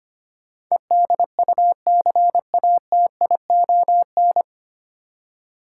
Morse code, heard as a keyed tone, about 29 wpm, 690 Hz. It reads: EDUCATION
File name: Easy Words - Speed-Racing with Hints - 25wpm